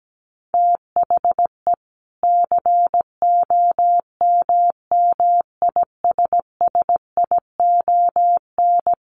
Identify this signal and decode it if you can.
Morse code, heard as a keyed tone, about 17 wpm, 700 Hz. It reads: THE COMMISSION
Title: Sentences from Top 1400 Words 17wpm